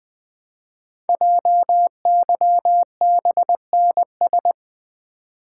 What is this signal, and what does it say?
Morse code, heard as a keyed tone, about 20 wpm, 700 Hz. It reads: JYBNS
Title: Code Groups - 5 Letters - 20wpm